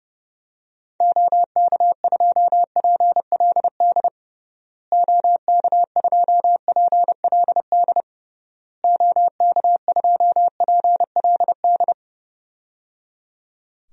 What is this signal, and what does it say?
Morse code, heard as a keyed tone, about 30 wpm, 700 Hz. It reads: OK2PLD OK2PLD OK2PLD
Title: DXCC-100 Call Signs - 6 Characters - Repeated 3x - 30wpm